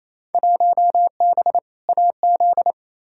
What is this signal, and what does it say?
Morse code, heard as a keyed tone, about 28 wpm, 700 Hz. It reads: 1B AZ
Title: Field Day Exchanges Encore - 28wpm